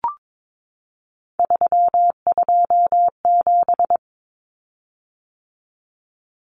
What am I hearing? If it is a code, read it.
Morse code, heard as a keyed tone, about 22 wpm, 700 Hz. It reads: 327
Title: Common Cut Numbers - 3-Digit 22wpm